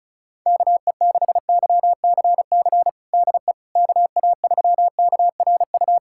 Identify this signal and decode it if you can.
Morse code, heard as a keyed tone, about 36 wpm, 700 Hz. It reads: KE6YCC DE KA3KRU
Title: USA Callsign DE Callsign 35wpm